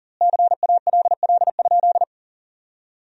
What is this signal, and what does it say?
Morse code, heard as a keyed tone, about 40 wpm, 700 Hz. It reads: CALL?
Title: 2x Word-Spacing - Sentences from Top 100 Words - 40wpm